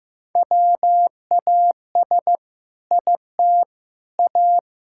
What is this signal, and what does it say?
Morse code, heard as a keyed tone, about 15 wpm, 700 Hz. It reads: WAS IT A
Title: Sentences from Top 1000 Words - Review Part 6 - 15wpm